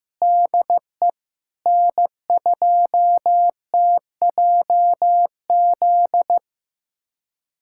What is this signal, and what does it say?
Morse code, heard as a keyed tone, about 15 wpm, 700 Hz. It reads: DE N2TJZ
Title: USA Callsign DE Callsign 15wpm